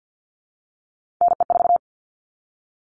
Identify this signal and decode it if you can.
Morse code, heard as a keyed tone, about 50 wpm, 700 Hz. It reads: NE4
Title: All Cut Numbers - 3-Digit 50wpm